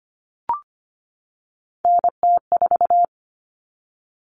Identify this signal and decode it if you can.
Morse code, heard as a keyed tone, about 25 wpm, 700 Hz. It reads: NT4
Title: All Cut Numbers - 3-Digit 25wpm